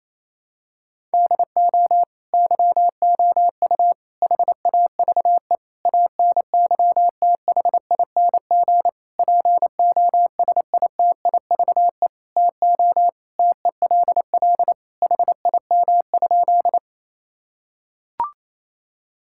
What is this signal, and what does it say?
Morse code, heard as a keyed tone, about 27 wpm, 700 Hz. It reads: DO YOU HAVE ANYTHING POSITIVE TO TELL HIM?
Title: Sentences from Top 1000 Words - Review Part 5 - 28wpm